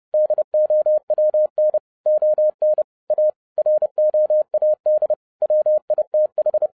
Morse code, heard as a keyed tone, about 30 wpm, 610 Hz. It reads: DOWN ON A ROAD WITH
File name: 2011-07-mp3 / n55 BACK FROM A BIG ACCIDENT 30wpm